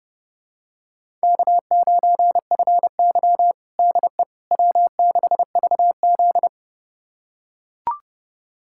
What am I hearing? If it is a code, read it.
Morse code, heard as a keyed tone, about 30 wpm, 700 Hz. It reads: K9FY DE W6VZ
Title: USA Callsign DE Callsign - 4 Characters - 30wpm